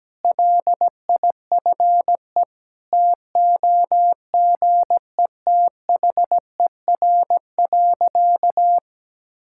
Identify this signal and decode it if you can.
Morse code, heard as a keyed tone, about 17 wpm, 700 Hz. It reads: LIFE TOGETHER.
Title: Sentences from Top 1000 Words - Review Part 5 - 17wpm